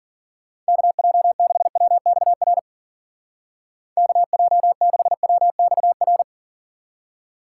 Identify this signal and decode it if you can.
Morse code, heard as a keyed tone, about 51 wpm, 700 Hz. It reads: KJ6WXR KJ6WXR